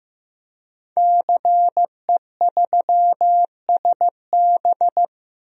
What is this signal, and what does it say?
Morse code, heard as a keyed tone, about 15 wpm, 700 Hz. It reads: CE3SB
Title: DXCC-100 Call Signs - 5 Characters - Repeated 2x - 15wpm